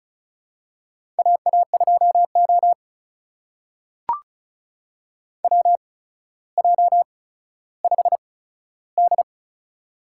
Morse code, heard as a keyed tone, about 36 wpm, spaced as for 15 wpm, 700 Hz. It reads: AA2O WJ5D
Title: USA Call Signs - 8x Character Spacing - 35wpm